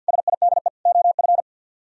Morse code, heard as a keyed tone, about 64 wpm, 700 Hz. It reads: SIDE OF